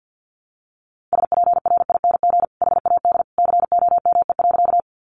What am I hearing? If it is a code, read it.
Morse code, heard as a keyed tone, about 50 wpm, 700 Hz. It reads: SPRING HAD COME.
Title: Sentences from Top 1200 Words 50wpm